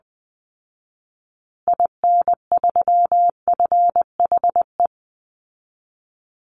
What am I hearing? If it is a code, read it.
Morse code, heard as a keyed tone, about 19 wpm, 700 Hz. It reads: IN3FHE
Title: Call Signs Repeated 2x - 20wpm